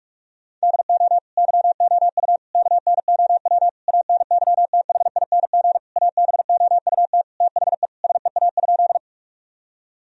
Morse code, heard as a keyed tone, about 46 wpm, 700 Hz. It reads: DO YOU KNOW ANYTHING ABOUT THE SEA?
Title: Sentences from Top 1000 Words - Review Part 1 - 45wpm